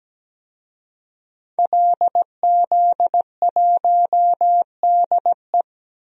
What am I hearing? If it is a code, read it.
Morse code, heard as a keyed tone, about 17 wpm, 700 Hz. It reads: LZ1DE
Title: DXCC-100 Call Signs - 5 Characters - 17wpm